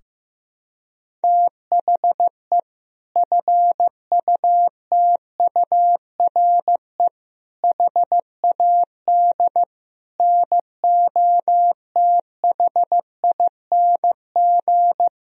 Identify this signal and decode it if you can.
Morse code, heard as a keyed tone, about 15 wpm, 700 Hz. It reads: THE FUTURE HAD NOTHING
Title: Sentences from Top 1000 Words - Review Part 4 - 15wpm